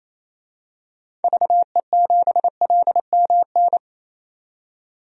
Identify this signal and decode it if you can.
Morse code, heard as a keyed tone, about 28 wpm, 700 Hz. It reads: VE7LMN
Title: DXCC-100 Call Signs - 6 Characters - 28wpm